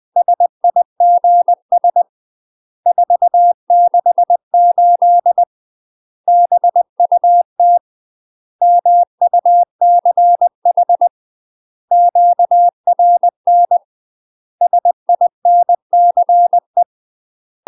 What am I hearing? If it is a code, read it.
Morse code, heard as a keyed tone, about 20 wpm, 700 Hz. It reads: SIGS 468 BUT MUCH QRN SINCE
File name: UK MASTER QSOs TRAINING / 20WPM / UK MASTER QSOs 0034